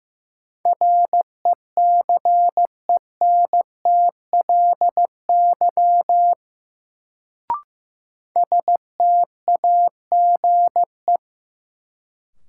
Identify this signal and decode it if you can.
Morse code, heard as a keyed tone, about 15 wpm, 700 Hz. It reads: RECENTLY STAGE
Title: Sentences from Top 1200 Words - New Word Drill Alt - 15wpm